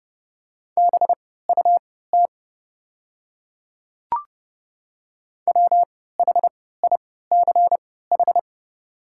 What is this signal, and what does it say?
Morse code, heard as a keyed tone, about 30 wpm, spaced as for 17 wpm, 700 Hz. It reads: BUT WHICH
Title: 3x Character Spacing - Top 100 Words - 30wpm